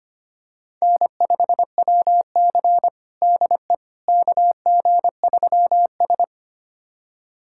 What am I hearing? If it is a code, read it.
Morse code, heard as a keyed tone, about 25 wpm, 700 Hz. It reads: N5WC DE KG3S
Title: USA Callsign DE Callsign - 4 Characters - 25wpm